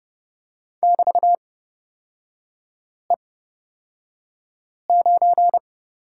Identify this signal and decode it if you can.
Morse code, heard as a keyed tone, about 30 wpm, spaced as for 5 wpm, 700 Hz. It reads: <BT>E9
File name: Lesson 36e - BT - Three Characters 30wpm